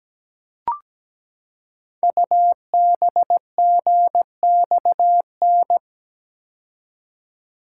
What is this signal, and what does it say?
Morse code, heard as a keyed tone, about 17 wpm, 700 Hz. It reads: UBGXN
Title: Code Groups - 5 Letters - 17wpm